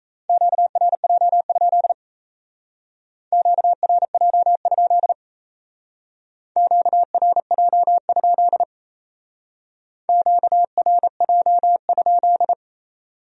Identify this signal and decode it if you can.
Morse code, heard as a keyed tone, about 36 wpm, 700 Hz. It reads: QRJ? QRJ? QRJ? QRJ?